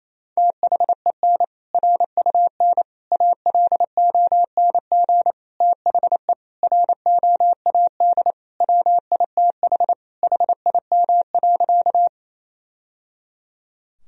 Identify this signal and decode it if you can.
Morse code, heard as a keyed tone, about 28 wpm, 700 Hz. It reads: THEN RUN ALONG THE ROAD WITH HIM.
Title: Sentences from Top 1000 Words - Review Part 2 - 28wpm